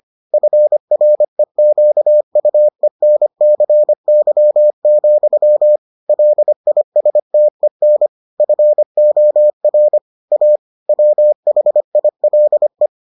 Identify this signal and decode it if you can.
Morse code, heard as a keyed tone, about 25 wpm, 600 Hz. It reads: FREQUENCY, LISTEN FOR A WHILE